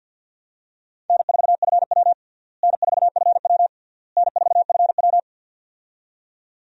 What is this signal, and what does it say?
Morse code, heard as a keyed tone, about 46 wpm, 700 Hz. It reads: N4FW N4FW N4FW
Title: USA Call Signs - 4 Characters - Repeated 3x - 50wpm